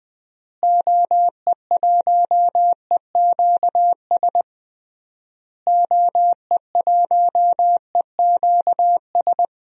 Morse code, heard as a keyed tone, about 20 wpm, 700 Hz. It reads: OE1EQS OE1EQS
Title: DXCC-100 Call Signs - 6 Characters - Repeated 3x - 20wpm